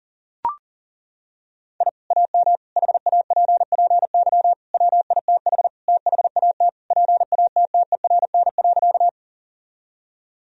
Morse code, heard as a keyed tone, about 37 wpm, 700 Hz. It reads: I AM HAPPY WITH THAT PATTERN.
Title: Sentences from Top 1000 Words - Review Part 2 - 40wpm